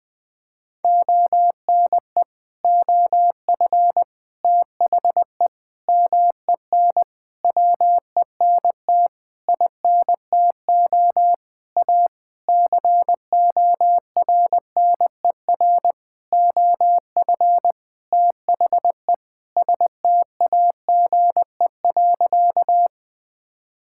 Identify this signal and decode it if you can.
Morse code, heard as a keyed tone, about 20 wpm, 700 Hz. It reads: ONE OF THE MEN WENT INTO A CORNER OF THE STAGE.
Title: Sentences from Top 1600 Words 20wpm